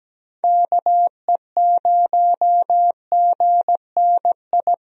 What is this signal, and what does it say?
Morse code, heard as a keyed tone, about 17 wpm, 700 Hz. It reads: KE0GNI
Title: USA Call Signs - 6 Characters - Repeated 2x - 17wpm